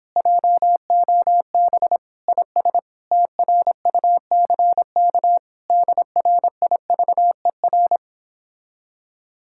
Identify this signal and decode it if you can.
Morse code, heard as a keyed tone, about 31 wpm, 700 Hz. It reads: JOB IS TRUCK DRIVER